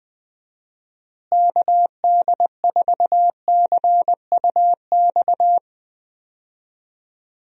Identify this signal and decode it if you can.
Morse code, heard as a keyed tone, about 20 wpm, 700 Hz. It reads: KD4CUX
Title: USA Call Signs - 6 Characters - Repeated 2x - 20wpm